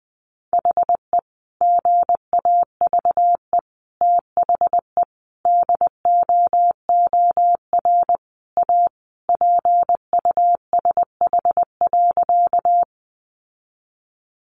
Morse code, heard as a keyed tone, about 20 wpm, 700 Hz. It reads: HE GAVE THE DOOR A PUSH.
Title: Sentences from Top 1300 Words 20wpm